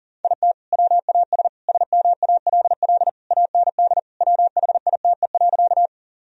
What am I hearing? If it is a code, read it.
Morse code, heard as a keyed tone, about 37 wpm, 700 Hz. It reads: IT WAS SMALL AND WHITE.
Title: Sentences from Top 1000 Words - Review Part 1 - 40wpm